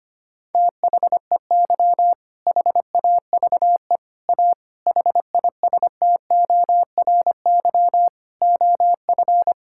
Morse code, heard as a keyed tone, about 25 wpm, 700 Hz. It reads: THEY HAVE A HISTORY OF
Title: Sentences from Top 1000 Words - Review Part 6 - 25wpm